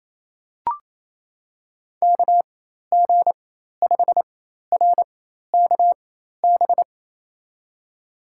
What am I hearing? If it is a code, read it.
Morse code, heard as a keyed tone, about 28 wpm, spaced as for 13 wpm, 700 Hz. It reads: KG5RKB
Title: USA Call Signs - 6 Characters - 4x Character Spacing - 28wpm